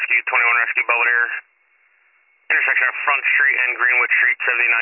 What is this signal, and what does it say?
Emergency Alert System (SAME) tones, heard as harsh buzzing bursts, with speech. It is heard from 1.1 to 4.5 s.